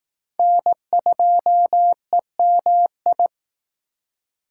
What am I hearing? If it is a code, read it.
Morse code, heard as a keyed tone, about 18 wpm, 700 Hz. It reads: N2EMI
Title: Speed Racing - USA Call Signs 15wpm